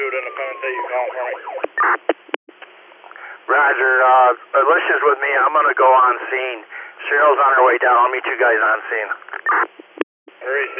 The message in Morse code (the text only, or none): none